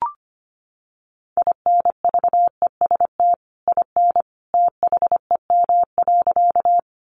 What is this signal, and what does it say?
Morse code, heard as a keyed tone, about 25 wpm, 700 Hz. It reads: INVEST IN THEM.
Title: Sentences from Top 1100 Words 25wpm